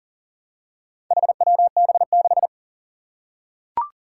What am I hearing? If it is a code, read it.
Morse code, heard as a keyed tone, about 41 wpm, 700 Hz. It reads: HWB6
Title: Four Challenging Letter-Number Combinations 40wpm